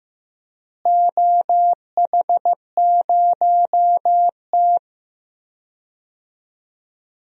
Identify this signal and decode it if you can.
Morse code, heard as a keyed tone, about 15 wpm, 700 Hz. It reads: OH0T
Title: DXCC-100 Call Signs - 4 Characters - Repeated 2x - 15wpm